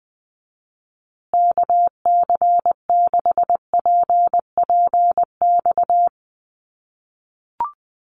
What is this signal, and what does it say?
Morse code, heard as a keyed tone, about 20 wpm, 700 Hz. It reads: KC6PPX
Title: USA Call Signs Repeated 3x 20wpm